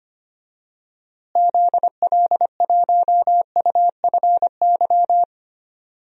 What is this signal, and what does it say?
Morse code, heard as a keyed tone, about 25 wpm, 700 Hz. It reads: ZL1UFY
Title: DXCC-100 Call Signs - 6 Characters - Repeated 2x - 25wpm